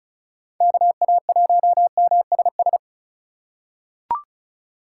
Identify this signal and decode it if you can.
Morse code, heard as a keyed tone, about 36 wpm, 700 Hz. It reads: KA1MSS
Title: USA Call Signs - 6x Character Spacing - 35wpm